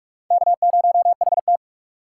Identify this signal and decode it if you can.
Morse code, heard as a keyed tone, about 46 wpm, 700 Hz. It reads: K0HT
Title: K1USN SST QSO Practice - 45wpm